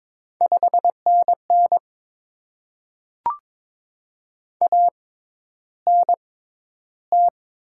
Morse code, heard as a keyed tone, about 22 wpm, spaced as for 11 wpm, 700 Hz. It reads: 5NN ANT
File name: Top 20 QSO Elements - 6x Character Spacing - 22wpm